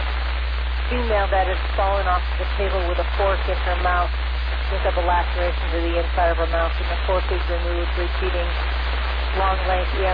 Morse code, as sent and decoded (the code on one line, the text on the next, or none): none